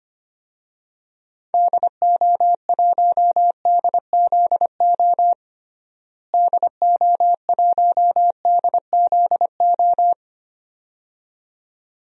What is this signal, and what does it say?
Morse code, heard as a keyed tone, about 25 wpm, 700 Hz. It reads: DO1DZO DO1DZO
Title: DXCC-100 Call Signs - 6 Characters - Repeated 2x - 25wpm